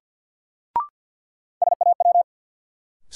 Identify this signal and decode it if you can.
Morse code, heard as a keyed tone, about 53 wpm, 700 Hz. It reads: SAW
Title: Sentences from Top 300 Words - New Word Drill - 50wpm